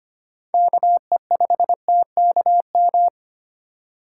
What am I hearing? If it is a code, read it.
Morse code, heard as a keyed tone, about 25 wpm, 700 Hz. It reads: KE5TKM
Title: USA Call Signs - 6 Characters - 4x Character Spacing - 25wpm